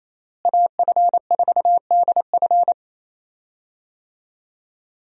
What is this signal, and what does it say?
Morse code, heard as a keyed tone, about 28 wpm, 700 Hz. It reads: AF4DF
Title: USA Call Signs - 5 Characters - Repeated 2x - 28wpm